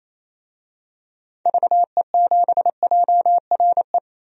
Morse code, heard as a keyed tone, about 28 wpm, 700 Hz. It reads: VE7JRE
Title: DXCC-100 Call Signs - 6 Characters - 28wpm